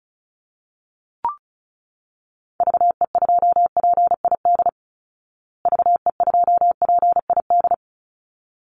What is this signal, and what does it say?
Morse code, heard as a keyed tone, about 34 wpm, 700 Hz. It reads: VE2PID VE2PID
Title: Call Signs Repeated 3x - 35wpm